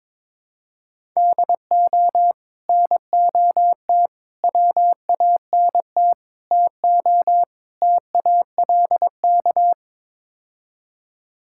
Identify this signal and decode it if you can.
Morse code, heard as a keyed tone, about 22 wpm, 700 Hz. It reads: DO NOT WANT TO TALK
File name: practice / 22wpm / Build A Phrase - 5-Word Phrases - 22wpm